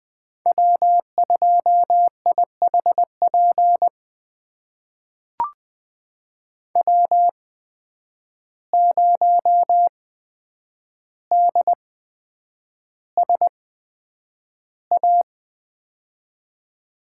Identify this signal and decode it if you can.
Morse code, heard as a keyed tone, about 20 wpm, spaced as for 9 wpm, 700 Hz. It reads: W2IHP W0DSA
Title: USA Call Signs - 5 Characters - 8x Character Spacing - 20wpm